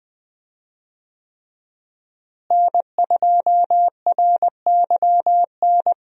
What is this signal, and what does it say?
Morse code, heard as a keyed tone, about 20 wpm, 700 Hz. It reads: N2RYN
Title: USA Call Signs - 5 Characters - Repeated 3x - 20wpm